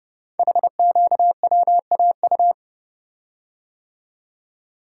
Morse code, heard as a keyed tone, about 31 wpm, 690 Hz. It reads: HQWAU
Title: Code Groups - 5 Letters - 30wpm